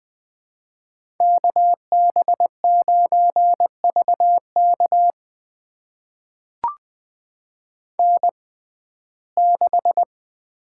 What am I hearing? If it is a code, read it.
Morse code, heard as a keyed tone, about 20 wpm, spaced as for 15 wpm, 700 Hz. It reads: KB9VK N6
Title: USA Call Signs - 5 Characters - 6x Character Spacing - 20wpm